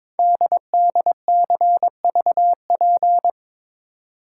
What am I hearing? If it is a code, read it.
Morse code, heard as a keyed tone, about 22 wpm, 700 Hz. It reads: DDCVP
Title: Code Groups - 5 Letters - 22wpm